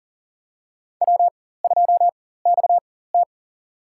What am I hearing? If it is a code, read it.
Morse code, heard as a keyed tone, about 40 wpm, spaced as for 19 wpm, 700 Hz. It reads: W2XT
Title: USA Call Signs - 4 Characters - 4x Character Spacing - 40wpm